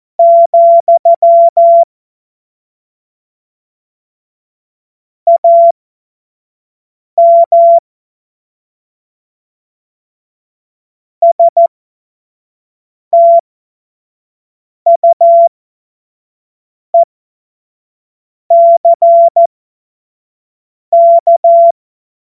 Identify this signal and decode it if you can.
Morse code, heard as a keyed tone, about 14 wpm, spaced as for 5 wpm, 680 Hz. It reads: , AM STUECK
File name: Morsekurs / Witze / Witz 12. Oktober 2021, 25 F70